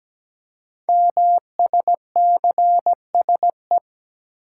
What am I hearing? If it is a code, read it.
Morse code, heard as a keyed tone, about 17 wpm, 700 Hz. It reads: MSCSE
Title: Code Groups - 5 Letter:Numbers - 17wpm